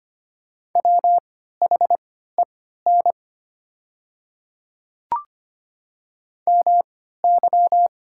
Morse code, heard as a keyed tone, about 25 wpm, spaced as for 14 wpm, 690 Hz. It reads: WHEN MY
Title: 3x Character Spacing - Top 100 Words - 25wpm